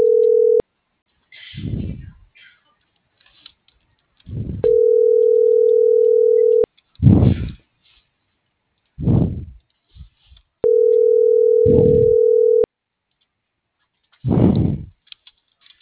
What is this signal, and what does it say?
Ringback tone (North America), 440 + 480 Hz.
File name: CENTRAL GOMSA / out garcia